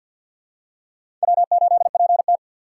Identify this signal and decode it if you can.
Morse code, heard as a keyed tone, about 53 wpm, 700 Hz. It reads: W8PT
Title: USA Call Signs - 4x Character Spacing - 50wpm